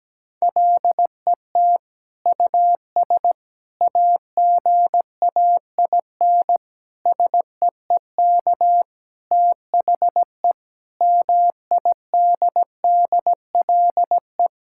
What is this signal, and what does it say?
Morse code, heard as a keyed tone, about 17 wpm, 700 Hz. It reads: LET US AGAIN SEEK THE MIDDLE
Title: Sentences from Top 1600 Words 17wpm